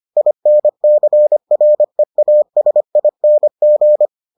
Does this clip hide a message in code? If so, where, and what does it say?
Yes: Morse code from 0.2 to 4.1 s, heard as a keyed tone, about 25 wpm, 600 Hz. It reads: INCREASING